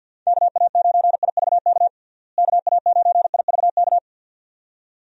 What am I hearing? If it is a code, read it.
Morse code, heard as a keyed tone, about 50 wpm, 700 Hz. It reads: KA9IVK KA9IVK
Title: USA Call Signs - 6 Characters - Repeated 2x - 50wpm